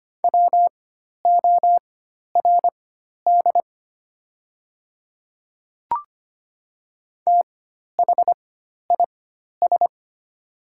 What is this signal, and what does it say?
Morse code, heard as a keyed tone, about 25 wpm, spaced as for 12 wpm, 700 Hz. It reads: WORD THIS
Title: 4x Character Spacing - Top 100 Words - 25wpm